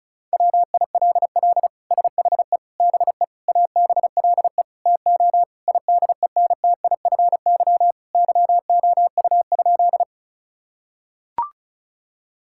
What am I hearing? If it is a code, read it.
Morse code, heard as a keyed tone, about 36 wpm, 700 Hz. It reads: WILL SHE BE ABLE TO IDENTIFY YOU?
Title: Sentences from Top 1000 Words - Review Part 6 - 35wpm